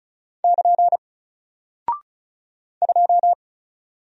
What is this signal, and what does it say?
Morse code, heard as a keyed tone, about 36 wpm, 700 Hz. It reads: <KN> 2
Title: Single Character (Letter:Number:Punctuation:Prosign) Alt 35wpm